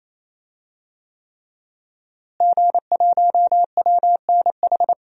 Morse code, heard as a keyed tone, about 28 wpm, 700 Hz. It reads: G1WNH
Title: DXCC-100 Call Signs - 5 Characters - Repeated 3x - 28wpm